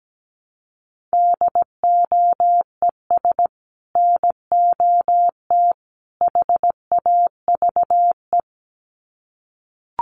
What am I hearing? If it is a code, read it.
Morse code, heard as a keyed tone, about 17 wpm, 700 Hz. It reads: DOES NOT HAVE
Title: Sets of 3 Words 17wpm